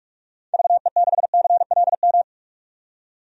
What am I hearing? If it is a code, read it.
Morse code, heard as a keyed tone, about 46 wpm, 700 Hz. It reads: VE6CLM
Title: DXCC-100 Call Signs - 6 Characters - 45wpm